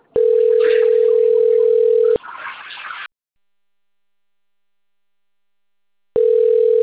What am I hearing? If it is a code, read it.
Ringback tone (North America), 440 + 480 Hz.
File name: rg-1600-unknown- - - .448